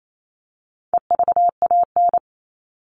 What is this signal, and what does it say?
Morse code, heard as a keyed tone, about 28 wpm, 700 Hz. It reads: EVAN
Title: All Cut Numbers - 4-Digit 28wpm